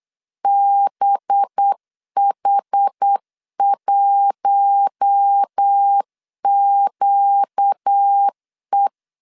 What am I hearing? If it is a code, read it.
Morse code, heard as a keyed tone, about 8 wpm, 800 Hz. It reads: BH1QE
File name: LEZIONI / 8 YJ45 / 8 Y 8wpm 3245